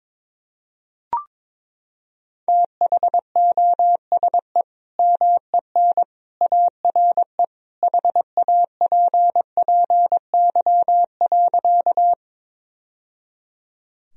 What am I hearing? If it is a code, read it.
Morse code, heard as a keyed tone, about 22 wpm, 700 Hz. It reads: THOSE MEN ARE HAPPY.
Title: Sentences from Top 1000 Words - Review Part 1 - 22wpm